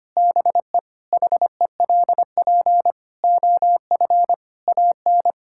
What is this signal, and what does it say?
Morse code, heard as a keyed tone, about 25 wpm, 700 Hz. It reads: BE HELP OF AN
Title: Sentences from Top 1000 Words - Review Part 5 - 25wpm